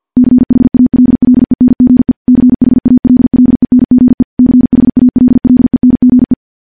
Morse code, heard as a keyed tone, about 50 wpm, 250 Hz. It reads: CHALLENGE CHALLENGE CHALLENGE
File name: morse / morse-training-50WPM